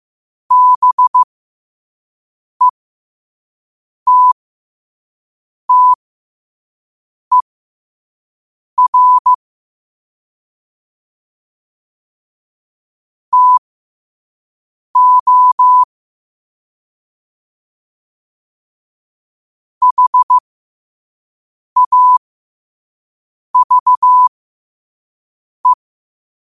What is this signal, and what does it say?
Morse code, heard as a keyed tone, about 15 wpm, spaced as for 5 wpm, 1000 Hz. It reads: BETTER TO HAVE